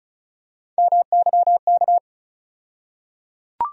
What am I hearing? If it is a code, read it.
Morse code, heard as a keyed tone, about 36 wpm, 700 Hz. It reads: MYK
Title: USA Call Sign Postfixes - 3 Letters 35wpm